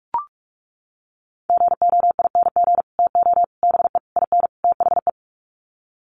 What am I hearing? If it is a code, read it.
Morse code, heard as a keyed tone, about 44 wpm, 700 Hz. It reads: GOING TO BE IN THE
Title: Sets of 5 Words Encore 45wpm